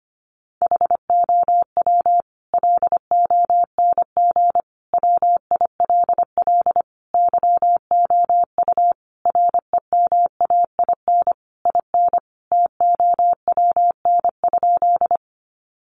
Morse code, heard as a keyed tone, about 25 wpm, 700 Hz. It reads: HOW LONG WILL YOU REMAIN IN TOWN?
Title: Sentences from Top 1200 Words 25wpm